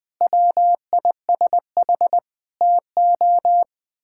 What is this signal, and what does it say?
Morse code, heard as a keyed tone, about 20 wpm, 700 Hz. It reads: WISH TO
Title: Sentences with 7-Word Phrases - Spoken Beginning - 20wpm